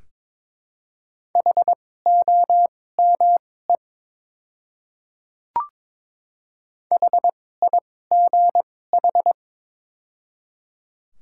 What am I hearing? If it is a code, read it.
Morse code, heard as a keyed tone, about 23 wpm, spaced as for 16 wpm, 700 Hz. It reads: HOME HIGH
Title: 2x Character Spacing - Top 200 Words - 22wpm